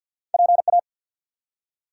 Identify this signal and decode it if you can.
Morse code, heard as a keyed tone, about 50 wpm, 700 Hz. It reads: PA
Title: US States and Canadian Provinces 50wpm